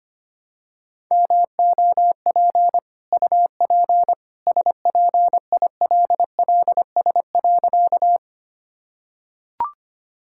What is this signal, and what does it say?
Morse code, heard as a keyed tone, about 25 wpm, 700 Hz. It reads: MOP UP SPILLS.
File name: Easy Sentences with Hints - 25wpm